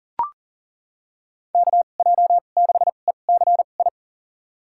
Morse code, heard as a keyed tone, about 40 wpm, spaced as for 29 wpm, 700 Hz. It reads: KJ6ECI
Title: USA Call Signs - 6 Characters - 2x Character Spacing - 40wpm